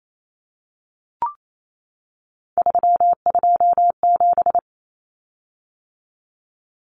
Morse code, heard as a keyed tone, about 28 wpm, 700 Hz. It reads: 327
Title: Common Cut Numbers - 3-Digit 28wpm